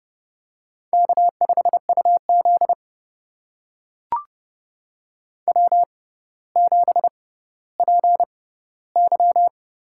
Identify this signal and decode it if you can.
Morse code, heard as a keyed tone, about 32 wpm, spaced as for 16 wpm, 700 Hz. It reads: K5UZ W7PY
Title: USA Call Signs - 4 Characters - 6x Character Spacing - 30wpm